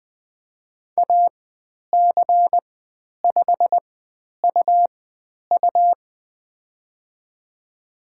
Morse code, heard as a keyed tone, about 20 wpm, spaced as for 10 wpm, 700 Hz. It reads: AC5UU
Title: 109 Lesson 18g - Letter m - Call Signs 20wpm